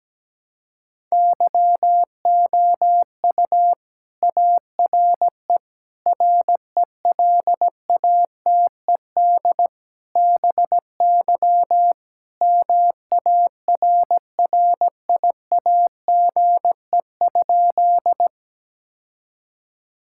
Morse code, heard as a keyed tone, about 17 wpm, 700 Hz. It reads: YOU ARE RELATED BY MARRIAGE?
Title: Sentences from Top 1000 Words - Review Part 5 - 17wpm